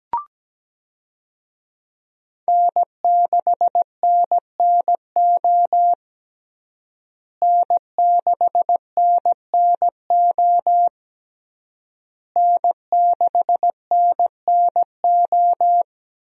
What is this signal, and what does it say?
Morse code, heard as a keyed tone, about 17 wpm, 700 Hz. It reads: N6NNO N6NNO N6NNO
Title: USA Call Signs - 5 Characters - Repeated 3x - 17wpm